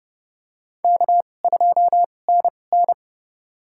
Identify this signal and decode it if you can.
Morse code, heard as a keyed tone, about 30 wpm, spaced as for 22 wpm, 700 Hz. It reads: K2NN
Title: USA Call Signs - 4 Characters - 2x Character Spacing - 30wpm